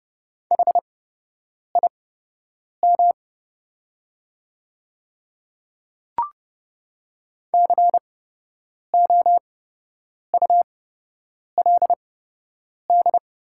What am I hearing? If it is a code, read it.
Morse code, heard as a keyed tone, about 30 wpm, spaced as for 8 wpm, 700 Hz. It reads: HIM COULD
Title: 8x Character Spacing - Top 100 Words - 30wpm